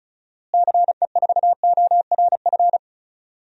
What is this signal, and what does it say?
Morse code, heard as a keyed tone, about 36 wpm, 700 Hz. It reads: CE4ORF
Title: DXCC-100 Call Signs - 6 Characters - Repeated 3x - 35wpm